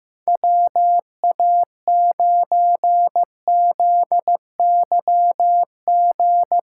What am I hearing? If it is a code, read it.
Morse code, heard as a keyed tone, about 15 wpm, 700 Hz. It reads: WA9ZYG
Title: USA Call Signs - 6 Characters - 2x Character Spacing - 15wpm